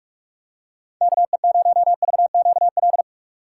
Morse code, heard as a keyed tone, about 46 wpm, 700 Hz. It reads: KE0VQL
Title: USA Call Signs - 6 Characters - 4x Character Spacing - 45wpm